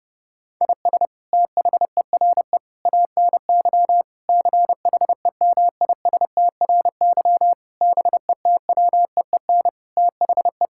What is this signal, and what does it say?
Morse code, heard as a keyed tone, about 30 wpm, 700 Hz. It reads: IS THERE ANY CHEMISTRY BETWEEN THE
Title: Sentences from Top 1000 Words - Review Part 6 - 30wpm